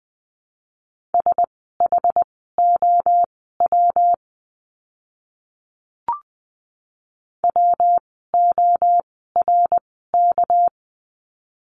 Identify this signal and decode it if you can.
Morse code, heard as a keyed tone, about 20 wpm, spaced as for 14 wpm, 700 Hz. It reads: SHOW WORK
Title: 206 Lesson 33f - Letter j - Words 20wpm